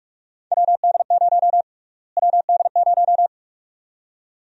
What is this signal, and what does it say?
Morse code, heard as a keyed tone, about 43 wpm, 700 Hz. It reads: WD0 WD0
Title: USA Call Sign Prefixes - Repeated 2x - 45wpm